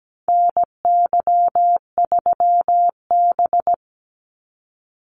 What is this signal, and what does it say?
Morse code, heard as a keyed tone, about 17 wpm, 700 Hz. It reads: NY3B
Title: Call Signs - 17wpm